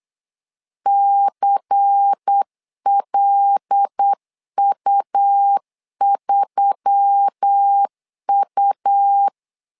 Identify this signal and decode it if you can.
Morse code, heard as a keyed tone, about 8 wpm, 800 Hz. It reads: CLU3U
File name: LEZIONI / 5 LBVC / 5 LBVC 8wpm 2935